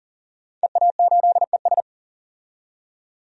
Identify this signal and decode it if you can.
Morse code, heard as a keyed tone, about 37 wpm, 700 Hz. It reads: EA8ES
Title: DXCC-100 Call Signs - 5 Characters - 40wpm